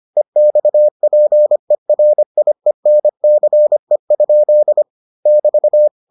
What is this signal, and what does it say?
Morse code, heard as a keyed tone, about 25 wpm, 600 Hz. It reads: EXPERIENCE? <BT>